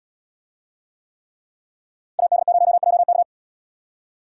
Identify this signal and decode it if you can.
Morse code, heard as a keyed tone, about 70 wpm, 700 Hz. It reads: AD0JK